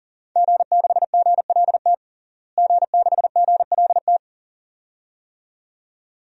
Morse code, heard as a keyed tone, about 40 wpm, 700 Hz. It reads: G6GLT G6GLT
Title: DXCC-100 Call Signs - 5 Characters - Repeated 2x - 40wpm